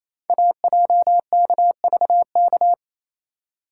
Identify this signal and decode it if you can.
Morse code, heard as a keyed tone, about 28 wpm, 700 Hz. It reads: AJKVK
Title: Code Groups - 5 Letters - 28wpm